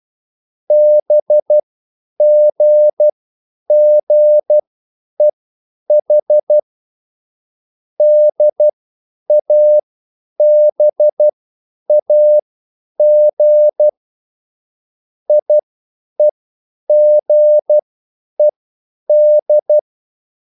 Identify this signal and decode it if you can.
Morse code, heard as a keyed tone, about 12 wpm, spaced as for 9 wpm, 600 Hz. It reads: BGGEH DABAG IEGED
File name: QSL / CW lesson